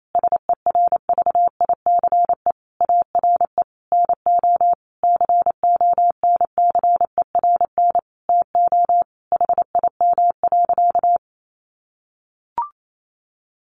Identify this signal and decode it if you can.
Morse code, heard as a keyed tone, about 28 wpm, 700 Hz. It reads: SERVICE ARE NO CONCERN TO HIM.
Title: Sentences from Top 1200 Words 28wpm